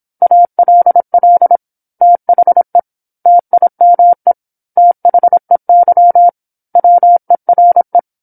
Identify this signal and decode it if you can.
Morse code, heard as a keyed tone, about 27 wpm, 700 Hz. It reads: ALL THE TIME THEY WERE
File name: 08-Chapter-08 26 WPM